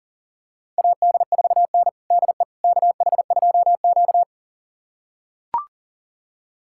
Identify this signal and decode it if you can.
Morse code, heard as a keyed tone, about 40 wpm, 700 Hz. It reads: AD4N DE KH2Q
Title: USA Callsign DE Callsign - 4 Characters - 40wpm